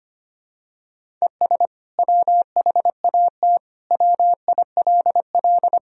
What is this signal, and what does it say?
Morse code, heard as a keyed tone, about 25 wpm, 700 Hz. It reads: ES WHAT WILL
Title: Sentences from Top 200 Words and Abbreviations 25wpm